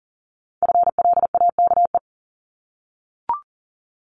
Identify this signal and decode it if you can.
Morse code, heard as a keyed tone, about 42 wpm, 700 Hz. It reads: FLAKE
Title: Multi-letter Phonemes 40wpm